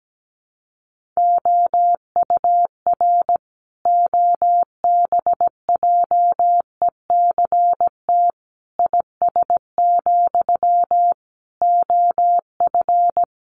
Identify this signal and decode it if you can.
Morse code, heard as a keyed tone, about 17 wpm, 700 Hz. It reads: OUR OBJECT IS, OF
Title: Sentences from Top 1500 Words 17wpm